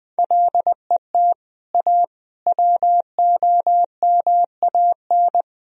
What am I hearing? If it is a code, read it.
Morse code, heard as a keyed tone, about 20 wpm, 700 Hz. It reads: LET A WOMAN
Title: Sentences from Top 1000 Words - Review Part 2 - 20wpm